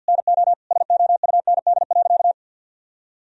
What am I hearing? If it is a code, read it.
Morse code, heard as a keyed tone, about 48 wpm, 700 Hz. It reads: NO SOUND.
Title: Sentences from Top 1000 Words - Review Part 2 - 50wpm